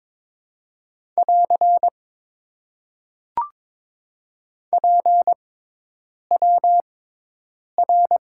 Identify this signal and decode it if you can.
Morse code, heard as a keyed tone, about 22 wpm, spaced as for 8 wpm, 700 Hz. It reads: <AR> PWR
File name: Top 90 QSO Elements - 6x Character Spacing - 22wpm